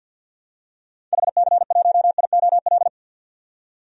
Morse code, heard as a keyed tone, about 51 wpm, 700 Hz. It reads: HC1IOL
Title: DXCC-100 Call Signs - 6 Characters - 50wpm